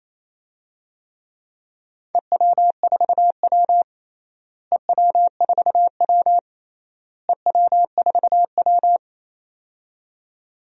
Morse code, heard as a keyed tone, about 27 wpm, 700 Hz. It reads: EW4W EW4W EW4W
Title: DXCC-100 Call Signs - 4 Characters - Repeated 3x - 28wpm